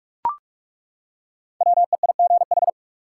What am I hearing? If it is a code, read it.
Morse code, heard as a keyed tone, about 41 wpm, 700 Hz. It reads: WEIGH
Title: Five Letter Words - 45wpm